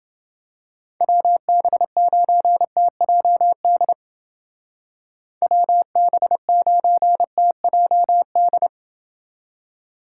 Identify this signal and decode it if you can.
Morse code, heard as a keyed tone, about 29 wpm, 700 Hz. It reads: WB9TJD WB9TJD